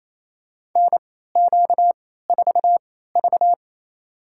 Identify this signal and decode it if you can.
Morse code, heard as a keyed tone, about 28 wpm, spaced as for 16 wpm, 700 Hz. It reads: NQ4V
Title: USA Call Signs - 3x Character Spacing - 28wpm